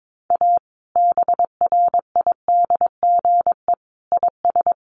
Morse code, heard as a keyed tone, about 22 wpm, 700 Hz. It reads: A BRIDGE IS
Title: Sentences from Top 1200 Words 22wpm